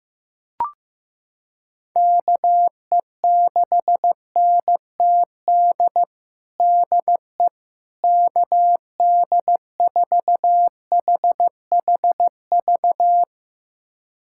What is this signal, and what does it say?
Morse code, heard as a keyed tone, about 16 wpm, 700 Hz. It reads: KE6NTD DE KD4HHV
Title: USA Callsign DE Callsign 15wpm